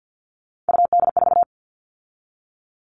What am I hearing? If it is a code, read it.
Morse code, heard as a keyed tone, about 50 wpm, 700 Hz. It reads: UD4
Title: All Cut Numbers - 3-Digit 50wpm